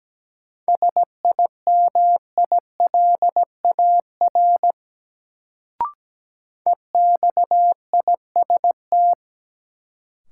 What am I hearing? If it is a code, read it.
Morse code, heard as a keyed tone, about 17 wpm, 700 Hz. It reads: SIMILAR EXIST
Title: Sentences from Top 1000 Words - Review Part 5 - New Word Drill - 17wpm